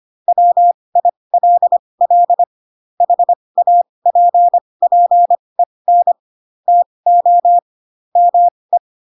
Morse code, heard as a keyed tone, about 25 wpm, spaced as for 20 wpm, 700 Hz. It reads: WILL HAPPEN TO ME
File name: Pinocchio / Pinocchio ch4 15 25 ik1hge